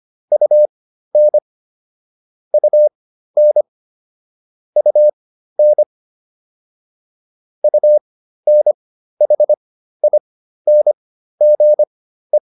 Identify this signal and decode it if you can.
Morse code, heard as a keyed tone, about 25 wpm, spaced as for 13 wpm, 600 Hz. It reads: UN UN UN UNHINGE
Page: intermediate-practice-files